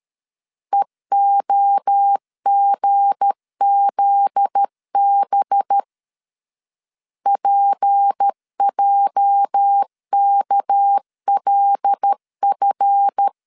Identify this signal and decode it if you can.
Morse code, heard as a keyed tone, about 13 wpm, 800 Hz. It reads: EOGZB PJKLF
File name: LEZIONI / 8 YJ45 / 8 YJ 12wpm 2235